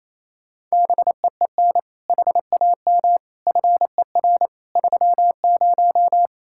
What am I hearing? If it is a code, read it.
Morse code, heard as a keyed tone, about 36 wpm, 700 Hz. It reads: BEEN HAM FER 30